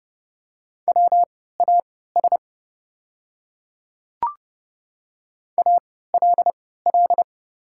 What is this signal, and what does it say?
Morse code, heard as a keyed tone, about 30 wpm, spaced as for 17 wpm, 690 Hz. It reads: WAS ALL
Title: 3x Character Spacing - Top 300 Words - 30wpm